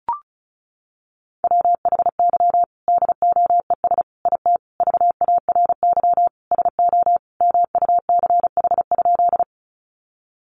Morse code, heard as a keyed tone, about 34 wpm, 700 Hz. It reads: WHY DOES IT VARY SO MUCH?
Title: Sentences from Top 1200 Words 35wpm